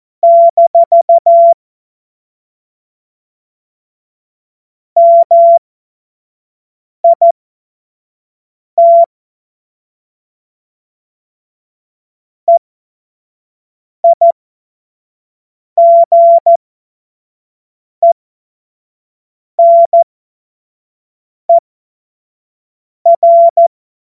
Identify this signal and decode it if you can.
Morse code, heard as a keyed tone, about 14 wpm, spaced as for 5 wpm, 680 Hz. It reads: - MIT EIGENER
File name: Morsekurs / Witze / Witz 13. Juni 2023 25 F70